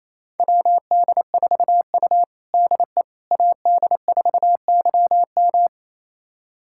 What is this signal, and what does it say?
Morse code, heard as a keyed tone, about 28 wpm, 700 Hz. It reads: WD4U DE AD4YM
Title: USA Callsign DE Callsign 28wpm